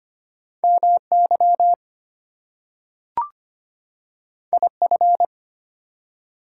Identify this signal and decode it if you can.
Morse code, heard as a keyed tone, about 25 wpm, 700 Hz. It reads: MY IF
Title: Two Letter Words - 25wpm